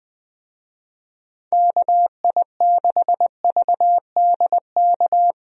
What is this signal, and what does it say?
Morse code, heard as a keyed tone, about 20 wpm, 700 Hz. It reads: KI6VDK
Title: USA Call Signs - 6 Characters - Repeated 2x - 20wpm